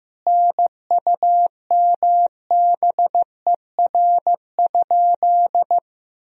Morse code, heard as a keyed tone, about 15 wpm, 700 Hz. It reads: NUMBER?
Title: 2x Word-Spacing - Sentences from Top 100 Words - 15wpm